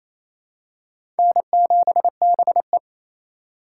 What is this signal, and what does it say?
Morse code, heard as a keyed tone, about 28 wpm, 700 Hz. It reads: N7BE
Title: USA Call Signs - 2x Character Spacing - 28wpm